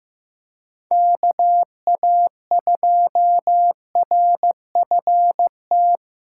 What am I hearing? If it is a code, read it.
Morse code, heard as a keyed tone, about 15 wpm, 700 Hz. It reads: KA2RFT
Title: USA Callsign DE Callsign 15wpm